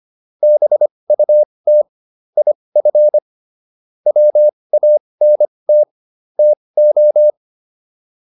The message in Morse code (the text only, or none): BUT IF WANT TO